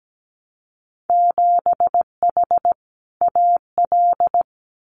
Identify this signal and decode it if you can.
Morse code, heard as a keyed tone, about 17 wpm, 700 Hz. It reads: 7H AL
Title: Winter Field Day Exchanges Encore 17wpm